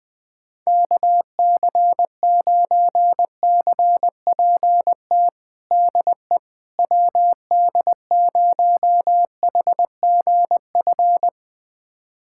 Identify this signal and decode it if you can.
Morse code, heard as a keyed tone, about 20 wpm, 700 Hz. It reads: KC9CPT DE WD0HGF
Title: USA Callsign DE Callsign - 6 Characters - 20wpm